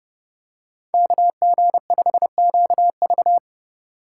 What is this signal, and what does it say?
Morse code, heard as a keyed tone, about 29 wpm, 700 Hz. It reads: KG5QV